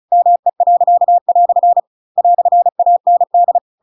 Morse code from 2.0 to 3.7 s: "AND